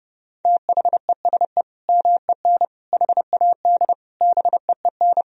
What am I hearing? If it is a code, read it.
Morse code, heard as a keyed tone, about 30 wpm, 700 Hz. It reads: THESE MEN HAD BEEN
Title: Sentences from Top 1000 Words - Review Part 1 - 30wpm